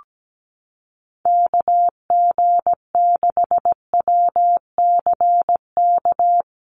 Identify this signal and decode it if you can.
Morse code, heard as a keyed tone, about 17 wpm, 700 Hz. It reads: KG6WCK
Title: USA Call Signs Repeated 3x 17wpm